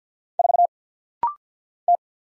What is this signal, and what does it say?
Morse code, heard as a keyed tone, about 50 wpm, 700 Hz. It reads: 4 T
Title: Single Letter-Number - Rapid-Fire 50wpm